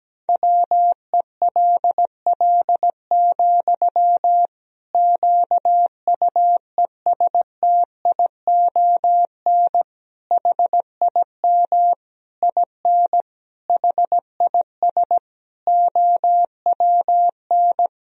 Morse code, heard as a keyed tone, about 17 wpm, 700 Hz. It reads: WELL, QUESTION HIM IN HIS OWN
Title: Sentences from Top 1000 Words - Review Part 3 - 17wpm